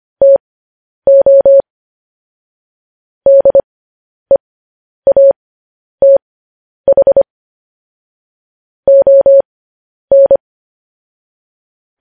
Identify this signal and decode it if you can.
Morse code, heard as a keyed tone, about 25 wpm, spaced as for 10 wpm, 550 Hz. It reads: TO DEATH ON